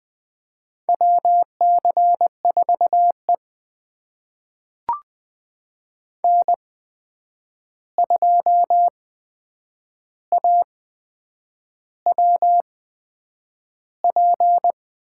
Morse code, heard as a keyed tone, about 20 wpm, spaced as for 8 wpm, 700 Hz. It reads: WC4E N2AWP
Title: USA Call Signs - 8x Character Spacing - 20wpm